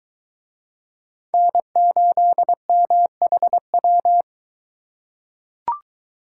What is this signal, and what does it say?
Morse code, heard as a keyed tone, about 22 wpm, 700 Hz. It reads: N8MHW